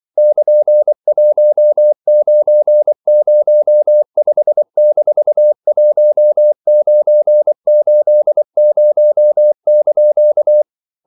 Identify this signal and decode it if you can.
Morse code, heard as a keyed tone, about 24 wpm, 600 Hz. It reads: <KN>1905-1980)
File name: CW / Practice / EnglishQuotations0324